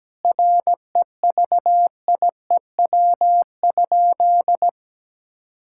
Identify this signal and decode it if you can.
Morse code, heard as a keyed tone, about 17 wpm, 700 Hz. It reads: REVIEW?
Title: Sentences from Top 1000 Words - Review Part 6 - 17wpm